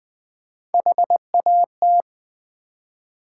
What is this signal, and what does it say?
Morse code, heard as a keyed tone, about 20 wpm, 700 Hz. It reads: HAT
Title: Three Letter Words - 20wpm